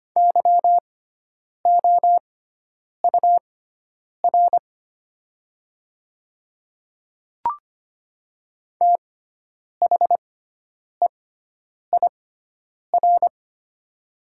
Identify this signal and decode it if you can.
Morse code, heard as a keyed tone, about 25 wpm, spaced as for 9 wpm, 700 Hz. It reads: YOUR THEIR
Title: 6x Character Spacing - Top 100 Words - 25wpm